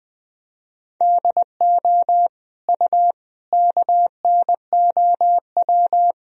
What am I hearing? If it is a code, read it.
Morse code, heard as a keyed tone, about 20 wpm, 700 Hz. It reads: DO U KNOW
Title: Sentences from Top 100 Words and Abbreviations 20wpm